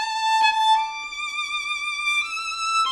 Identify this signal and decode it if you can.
There is no code.